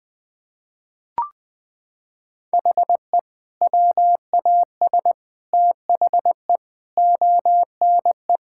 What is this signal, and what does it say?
Morse code, heard as a keyed tone, about 20 wpm, 700 Hz. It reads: HE WAS THE ONE
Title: Sentences with 2-Word Phrases - 20wpm